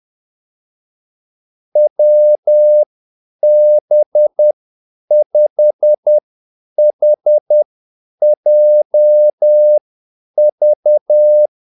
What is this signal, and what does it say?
Morse code, heard as a keyed tone, about 10 wpm, spaced as for 8 wpm, 600 Hz. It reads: WB5HJV
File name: Callsigns-10-8wpm